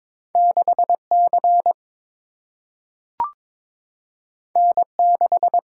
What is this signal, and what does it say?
Morse code, heard as a keyed tone, about 22 wpm, 700 Hz. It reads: 6C N6
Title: Single-Letter Single-Number 22wpm